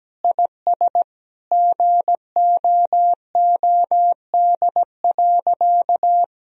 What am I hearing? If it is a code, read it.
Morse code, heard as a keyed tone, about 17 wpm, 700 Hz. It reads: IS GOOD.
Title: Sentences from Top 1000 Words - Review Part 3 - 17wpm